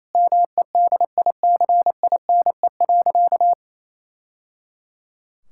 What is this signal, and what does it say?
Morse code, heard as a keyed tone, about 28 wpm, 700 Hz. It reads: MEDICINE.
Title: Sentences from Top 1000 Words - Review Part 6 - 28wpm